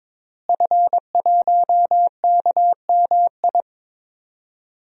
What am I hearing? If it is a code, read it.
Morse code, heard as a keyed tone, about 22 wpm, 700 Hz. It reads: F1KMI
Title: DXCC-100 Call Signs - 5 Characters - Repeated 2x - 22wpm